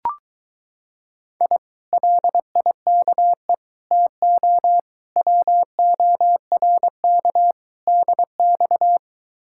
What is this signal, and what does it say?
Morse code, heard as a keyed tone, about 23 wpm, 700 Hz. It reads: I LIKE TO WORK DX